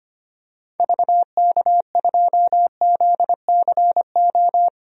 Morse code, heard as a keyed tone, about 25 wpm, 700 Hz. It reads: VK2ZCO
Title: DXCC-100 Call Signs - 6 Characters - 25wpm